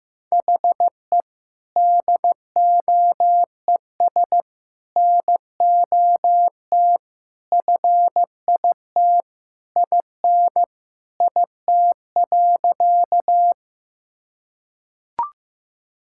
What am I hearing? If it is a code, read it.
Morse code, heard as a keyed tone, about 15 wpm, 700 Hz. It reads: HE DOES NOT FIT IN IT.
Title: Sentences from Top 1000 Words - Review Part 5 - 15wpm